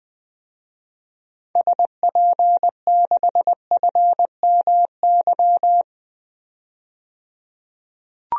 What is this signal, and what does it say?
Morse code, heard as a keyed tone, about 20 wpm, 700 Hz. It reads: SP6FMY
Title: DXCC-100 Call Signs - 6 Characters - 20wpm